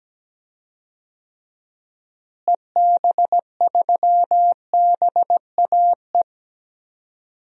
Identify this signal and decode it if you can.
Morse code, heard as a keyed tone, about 17 wpm, 700 Hz. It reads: EB3BAE
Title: DXCC-100 Call Signs - 6 Characters - Repeated 3x - 17wpm